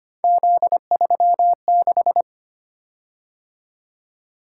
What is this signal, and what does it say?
Morse code, heard as a keyed tone, about 25 wpm, 700 Hz. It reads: Z36
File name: Three Letter-Number Combinations - 25wpm